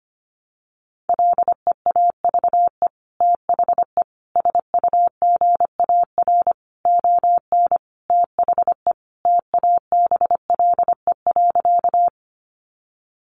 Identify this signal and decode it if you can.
Morse code, heard as a keyed tone, about 25 wpm, 700 Hz. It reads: LEAVE THE SUGAR ON THE TABLE.
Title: Sentences from Top 1200 Words 25wpm